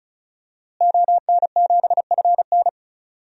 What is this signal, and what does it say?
Morse code, heard as a keyed tone, about 34 wpm, 700 Hz. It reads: ON7FN
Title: DXCC-100 Call Signs - 5 Characters - Repeated 2x - 35wpm